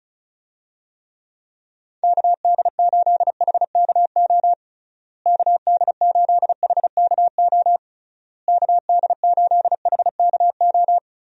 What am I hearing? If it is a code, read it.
Morse code, heard as a keyed tone, about 34 wpm, 700 Hz. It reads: KD8HKO KD8HKO KD8HKO
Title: USA Call Signs - 6 Characters - Repeated 3x - 35wpm